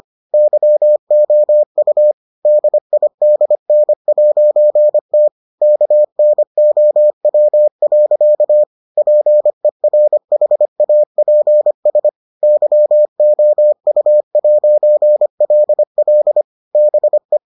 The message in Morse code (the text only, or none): YOU DIDN'T KNOW. PERHAPS YOU'LL BE